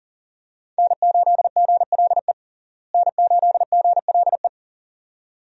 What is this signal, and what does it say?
Morse code, heard as a keyed tone, about 40 wpm, 700 Hz. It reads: N8GLE N8GLE
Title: USA Call Signs - 5 Characters - Repeated 2x - 40wpm